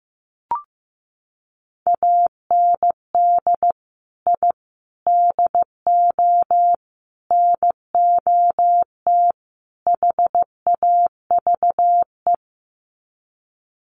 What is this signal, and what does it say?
Morse code, heard as a keyed tone, about 15 wpm, 700 Hz. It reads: AND I DO NOT HAVE
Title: Sets of 5 Words Encore 15wpm